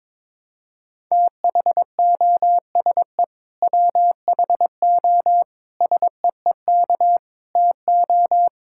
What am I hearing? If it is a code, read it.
Morse code, heard as a keyed tone, about 22 wpm, 700 Hz. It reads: THOSE WHO SEEK TO
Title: Sentences from Top 1600 Words 22wpm